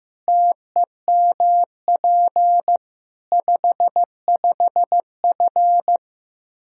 Morse code, heard as a keyed tone, about 15 wpm, 700 Hz. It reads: TEMP 55F
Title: K1USN SST QSO Practice 15wpm